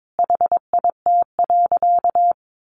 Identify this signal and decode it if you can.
Morse code, heard as a keyed tone, about 22 wpm, 700 Hz. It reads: HIT.
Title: Sentences from Top 1200 Words 22wpm